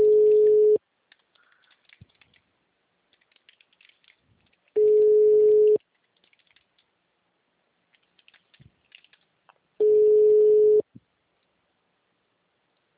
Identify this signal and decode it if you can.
Ringback tone (Europe, 425 Hz).